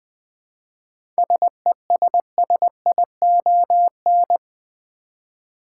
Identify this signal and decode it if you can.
Morse code, heard as a keyed tone, about 20 wpm, 700 Hz. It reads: SESSION
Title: Seven Letter Words - 20wpm